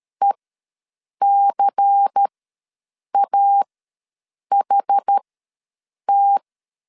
Morse code, heard as a keyed tone, about 13 wpm, spaced as for 7 wpm, 800 Hz. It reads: ECAHT
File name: LEZIONI / 6 PKX0 / 6 PKX0 ws 12wpm 3448